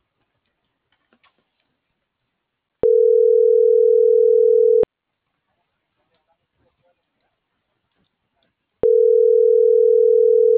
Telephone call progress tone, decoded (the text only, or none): ringback tone